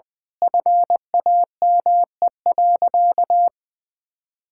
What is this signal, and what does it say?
Morse code, heard as a keyed tone, about 20 wpm, 700 Hz. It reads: FAME.
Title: Sentences with 6-Word Phrases - 20wpm